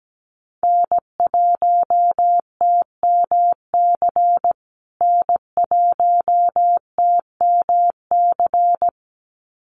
Morse code, heard as a keyed tone, about 17 wpm, 700 Hz. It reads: N1TMC N1TMC
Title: USA Call Signs Repeated 2x 17wpm